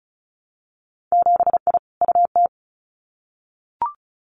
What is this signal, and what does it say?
Morse code, heard as a keyed tone, about 36 wpm, 700 Hz. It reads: 7I UT
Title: Winter Field Day Exchanges Encore 35wpm